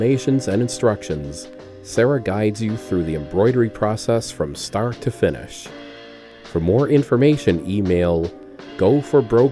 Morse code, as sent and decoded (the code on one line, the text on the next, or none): none